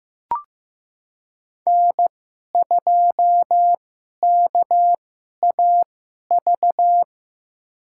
Morse code, heard as a keyed tone, about 15 wpm, spaced as for 11 wpm, 700 Hz. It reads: N2KAV
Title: USA Call Signs - 5 Characters - 2x Character Spacing - 15wpm